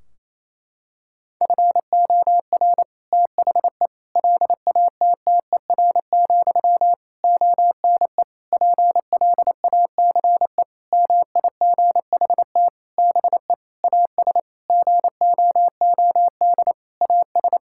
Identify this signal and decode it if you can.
Morse code, heard as a keyed tone, about 27 wpm, 700 Hz. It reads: FOR THE LATTER, ONE PLACE MIGHT BE AS GOOD AS
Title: Sentences from Top 1600 Words 28wpm